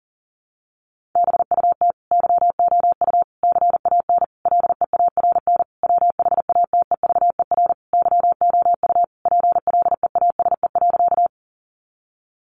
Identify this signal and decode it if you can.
Morse code, heard as a keyed tone, about 40 wpm, 700 Hz. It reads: BUT YOU CAN LEARN WHATEVER YOU PLEASE.
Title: Sentences from Top 1500 Words 40wpm